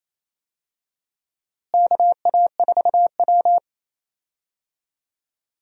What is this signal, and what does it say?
Morse code, heard as a keyed tone, about 28 wpm, 700 Hz. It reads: KA4W
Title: USA Call Signs - 4 Characters - Repeated 3x - 28wpm